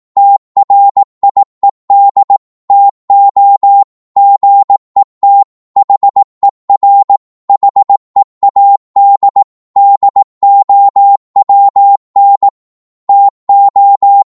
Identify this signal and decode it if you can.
Morse code, heard as a keyed tone, about 18 wpm, 800 Hz. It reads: TRIED TO GET HER HEAD DOWN TO
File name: 18wpm